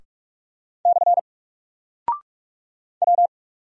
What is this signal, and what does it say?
Morse code, heard as a keyed tone, about 44 wpm, 700 Hz. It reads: / W
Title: Single Character (Letter:Number:Punctuation:Prosign) 45wpm